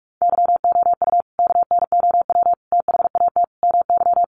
Morse code, heard as a keyed tone, about 44 wpm, 700 Hz. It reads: YOU KNOW THAT MY
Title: Sentences from Top 1400 Words 45wpm